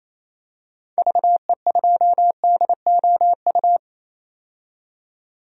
Morse code, heard as a keyed tone, about 27 wpm, 700 Hz. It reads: VE2DOU
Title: DXCC-100 Call Signs - 6 Characters - 28wpm